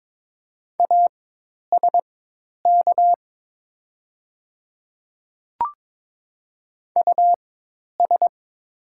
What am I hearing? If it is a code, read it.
Morse code, heard as a keyed tone, about 22 wpm, spaced as for 10 wpm, 700 Hz. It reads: ASK US
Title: 4x Character Spacing - Top 200 Words - 22wpm